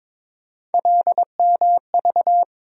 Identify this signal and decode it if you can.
Morse code, heard as a keyed tone, about 22 wpm, 700 Hz. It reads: LMV
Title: USA Call Sign Postfixes - 3 Letters 22wpm